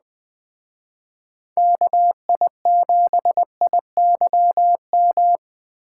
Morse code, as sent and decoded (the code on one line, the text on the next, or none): -.- .. --... .. -.-- --
KI7IYM